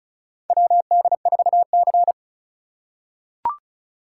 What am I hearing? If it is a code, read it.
Morse code, heard as a keyed tone, about 36 wpm, 700 Hz. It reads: WD4C
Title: USA Call Signs - 4 Characters - 4x Character Spacing - 35wpm